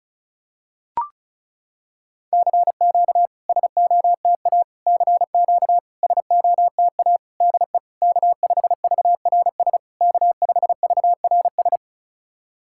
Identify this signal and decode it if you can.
Morse code, heard as a keyed tone, about 36 wpm, 690 Hz. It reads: CQ SOTA CQ SOTA DE K5VRS K5VRS
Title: SOTA QSO Practice 35wpm